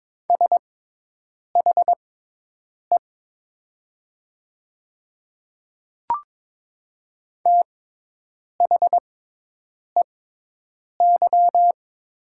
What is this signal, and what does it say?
Morse code, heard as a keyed tone, about 22 wpm, spaced as for 8 wpm, 700 Hz. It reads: SHE THEY
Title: 6x Character Spacing - Top 200 Words - 22wpm